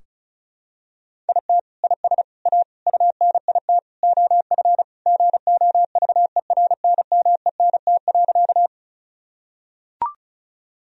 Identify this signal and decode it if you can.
Morse code, heard as a keyed tone, about 34 wpm, 700 Hz. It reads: IT IS A UNIT OF GOVERNMENT.
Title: Sentences from Top 1000 Words - Review Part 2 - 35wpm